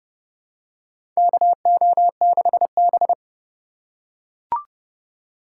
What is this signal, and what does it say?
Morse code, heard as a keyed tone, about 32 wpm, 700 Hz. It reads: KO6B
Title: USA Call Signs - 4 Characters - 4x Character Spacing - 30wpm